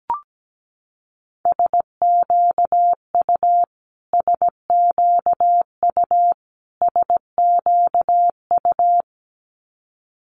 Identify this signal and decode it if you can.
Morse code, heard as a keyed tone, about 17 wpm, 700 Hz. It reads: SQU SQU SQU
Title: Multi-letter Phonemes Repeated 3x 17wpm